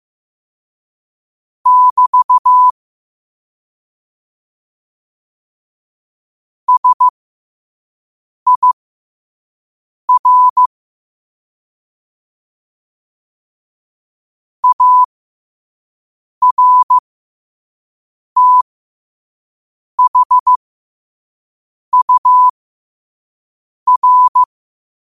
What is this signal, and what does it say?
Morse code, heard as a keyed tone, about 15 wpm, spaced as for 5 wpm, 1000 Hz. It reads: <BT> SIR ARTHUR
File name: qotdfiles / 5wpm